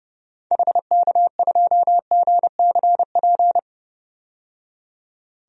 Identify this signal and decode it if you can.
Morse code, heard as a keyed tone, about 32 wpm, 700 Hz. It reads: HK2GCP
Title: DXCC-100 Call Signs - 6 Characters - 30wpm